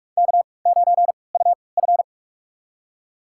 Morse code, heard as a keyed tone, about 41 wpm, spaced as for 24 wpm, 700 Hz. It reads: K9UF
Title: USA Call Signs - 4 Characters - 3x Character Spacing - 45wpm